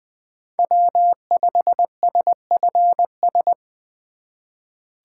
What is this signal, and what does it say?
Morse code, heard as a keyed tone, about 21 wpm, 700 Hz. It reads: W5SFS
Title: USA Call Signs - 5 Characters - 3x Character Spacing - 20wpm